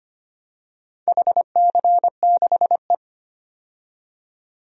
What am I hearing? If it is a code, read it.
Morse code, heard as a keyed tone, about 25 wpm, 700 Hz. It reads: HC6E
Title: DXCC-100 Call Signs - 4 Characters - 25wpm